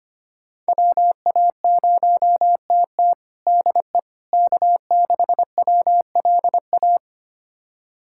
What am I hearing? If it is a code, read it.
Morse code, heard as a keyed tone, about 25 wpm, 700 Hz. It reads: WA0TT DE K6WLA
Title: USA Callsign DE Callsign - 5 Characters - 25wpm